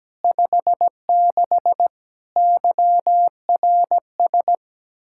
Morse code, heard as a keyed tone, about 17 wpm, 700 Hz. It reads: 56 YRS
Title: Practice QSO - Exchange 3 - More Info - 17wpm